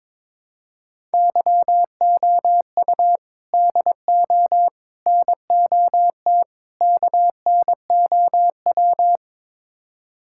Morse code, heard as a keyed tone, about 21 wpm, 700 Hz. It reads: YOU DO NOT KNOW
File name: practice / 22wpm / Build A Phrase - 5-Word Phrases - 22wpm